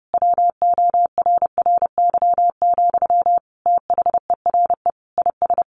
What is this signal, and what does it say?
Morse code, heard as a keyed tone, about 30 wpm, 700 Hz. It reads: WORRY, THERE IS
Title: Sentences from Top 1300 Words 30wpm